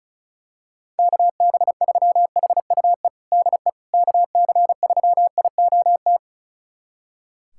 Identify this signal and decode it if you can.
Morse code, heard as a keyed tone, about 36 wpm, 700 Hz. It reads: KB3HUE DE KC3IOT
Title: USA Callsign DE Callsign - 6 Characters - 35wpm